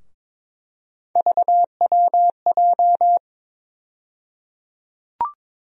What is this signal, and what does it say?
Morse code, heard as a keyed tone, about 22 wpm, 700 Hz. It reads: VWJ
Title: USA Call Sign Postfixes - 3 Letters 22wpm